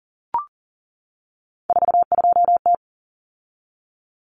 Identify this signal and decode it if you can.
Morse code, heard as a keyed tone, about 40 wpm, 700 Hz. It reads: 42T
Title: Common Cut Numbers - 3-Digit 40wpm